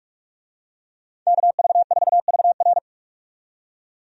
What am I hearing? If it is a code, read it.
Morse code, heard as a keyed tone, about 46 wpm, 700 Hz. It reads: KV4VR
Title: USA Call Signs - 5 Characters - 45wpm